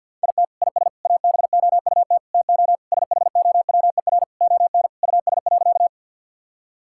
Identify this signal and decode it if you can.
Morse code, heard as a keyed tone, about 50 wpm, 700 Hz. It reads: IT IS ABOUT TO SHOWER ON US.
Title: Sentences from Top 1600 Words 50wpm